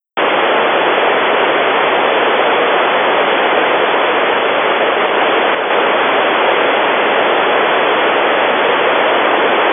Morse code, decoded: none